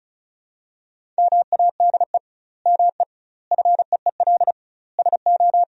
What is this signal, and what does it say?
Morse code, heard as a keyed tone, about 36 wpm, 700 Hz. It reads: MADE ME FEEL SO
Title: 2x Word-Spacing - Sentences from Top 400 Words - 35wpm